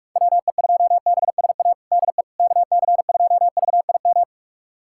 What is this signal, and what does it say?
Morse code, heard as a keyed tone, about 43 wpm, 700 Hz. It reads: WE2BSA DE KC2VIM
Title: USA Callsign DE Callsign - 6 Characters - 45wpm